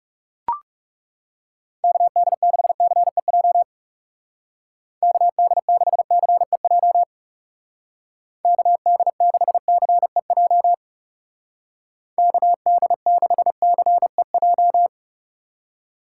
Morse code, heard as a keyed tone, about 36 wpm, 700 Hz. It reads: KD6CEJ KD6CEJ KD6CEJ KD6CEJ
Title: Speed Racing - USA 2x3 Call Signs 30wpm